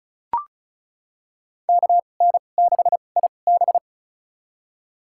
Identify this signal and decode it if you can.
Morse code, heard as a keyed tone, about 36 wpm, spaced as for 25 wpm, 700 Hz. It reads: KN6IB
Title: USA Call Signs - 5 Characters - 2x Character Spacing - 35wpm